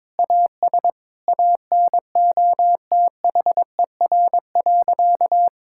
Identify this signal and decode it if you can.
Morse code, heard as a keyed tone, about 22 wpm, 700 Hz. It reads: AS ANOTHER.
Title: Sentences from Top 1600 Words 22wpm